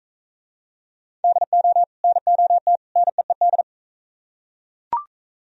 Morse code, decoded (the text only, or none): DO NOT NEED